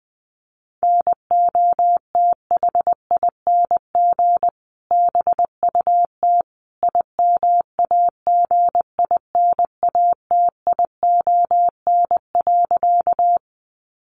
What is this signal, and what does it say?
Morse code, heard as a keyed tone, about 20 wpm, 700 Hz. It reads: NOTHING BUT IMAGINATION.
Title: Sentences from Top 1300 Words 20wpm